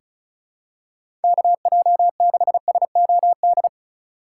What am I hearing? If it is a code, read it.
Morse code, heard as a keyed tone, about 36 wpm, 700 Hz. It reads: KJ6SOD
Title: USA Call Signs - 6 Characters - 3x Character Spacing - 35wpm